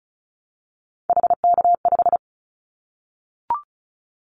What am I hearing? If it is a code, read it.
Morse code, heard as a keyed tone, about 34 wpm, 700 Hz. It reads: HK5
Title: Callsign Prefixes for Top 100 DXCC - 35wpm